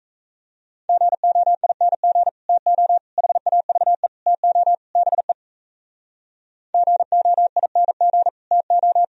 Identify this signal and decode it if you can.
Morse code, heard as a keyed tone, about 38 wpm, 700 Hz. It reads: GOING TO HAVE TO BE GOING TO